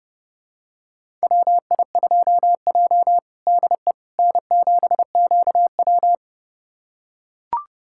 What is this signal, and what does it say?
Morse code, heard as a keyed tone, about 30 wpm, 700 Hz. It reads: WI2J DE N7QW
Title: USA Callsign DE Callsign - 4 Characters - 30wpm